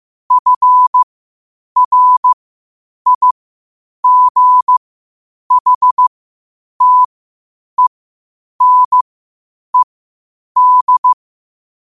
Morse code, heard as a keyed tone, about 15 wpm, spaced as for 8 wpm, 1000 Hz. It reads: FRIGHTENED
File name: qotdfiles / 8wpm